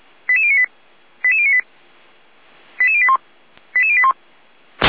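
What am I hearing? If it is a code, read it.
Five-tone selective call (ZVEI), heard as quick runs of five tones, tones of about 70 ms. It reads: 88098 88098 88081 88081
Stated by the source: Five-tone ZVEI call 88571 received on an SDR.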